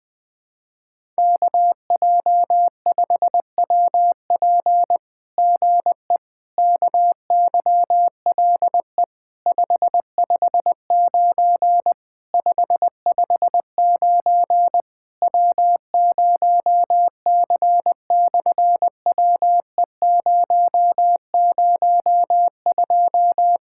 Morse code, heard as a keyed tone, about 20 wpm, 700 Hz. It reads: KJ5WP GE KYLE 559 559 W0C/WE002
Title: SOTA QSO Practice 20wpm